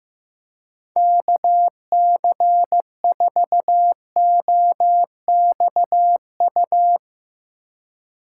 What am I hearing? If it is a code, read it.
Morse code, heard as a keyed tone, about 15 wpm, 700 Hz. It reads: KC4OXU
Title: USA Call Signs - 6 Characters - 8x Character Spacing - 15wpm